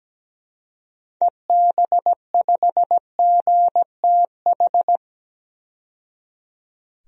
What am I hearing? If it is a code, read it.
Morse code, heard as a keyed tone, about 17 wpm, 700 Hz. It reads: EB5GTH
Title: DXCC-100 Call Signs - 6 Characters - 17wpm